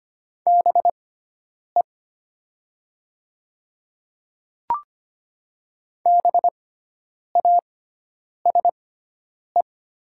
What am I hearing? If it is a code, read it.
Morse code, heard as a keyed tone, about 25 wpm, spaced as for 9 wpm, 700 Hz. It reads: BE BASE
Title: 6x Character Spacing - Top 300 Words - 25wpm